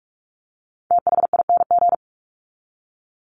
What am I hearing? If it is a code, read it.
Morse code, heard as a keyed tone, about 44 wpm, 700 Hz. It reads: THING
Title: Top 300 Words 45wpm